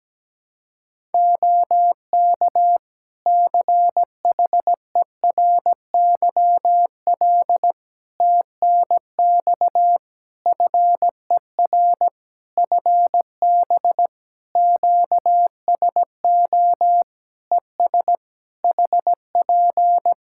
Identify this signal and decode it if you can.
Morse code, heard as a keyed tone, about 17 wpm, 700 Hz. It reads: OK CHERYL TNX FER FB QSO ES HP
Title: Practice QSO - Exchange 4 - Ending - 17wpm